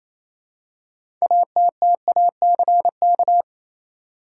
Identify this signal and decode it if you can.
Morse code, heard as a keyed tone, about 28 wpm, 700 Hz. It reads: ATTACK
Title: Six Letter Words - 28wpm